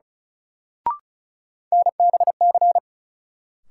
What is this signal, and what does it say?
Morse code, heard as a keyed tone, about 36 wpm, 700 Hz. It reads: NBC
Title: Sets of 5 Words Encore - New Word Drill - 35wpm